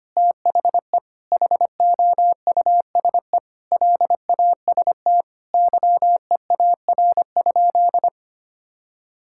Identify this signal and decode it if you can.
Morse code, heard as a keyed tone, about 25 wpm, 700 Hz. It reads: THE HOUSE LAST YEAR?
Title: Sentences from Top 1000 Words - Review Part 6 - 25wpm